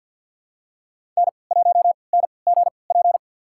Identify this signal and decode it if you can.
Morse code, heard as a keyed tone, about 52 wpm, spaced as for 29 wpm, 700 Hz. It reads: N1NGP
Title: USA Call Signs - 5 Characters - 3x Character Spacing - 50wpm